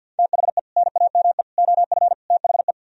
Morse code, heard as a keyed tone, about 64 wpm, 700 Hz. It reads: THE NAME OF THE